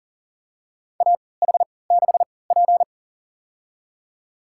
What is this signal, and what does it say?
Morse code, heard as a keyed tone, about 40 wpm, spaced as for 23 wpm, 700 Hz. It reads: AH6P
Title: USA Call Signs - 4 Characters - 3x Character Spacing - 40wpm